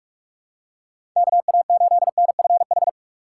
Morse code, heard as a keyed tone, about 46 wpm, 700 Hz. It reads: KA8NFH
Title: USA Call Signs - 6 Characters - 4x Character Spacing - 45wpm